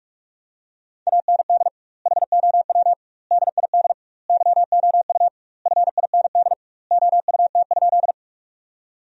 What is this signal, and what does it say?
Morse code, heard as a keyed tone, about 41 wpm, 700 Hz. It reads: AND HOW DID YOU FIND OUT?
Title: 2x Word-Spacing - Sentences from Top 100 Words - 45wpm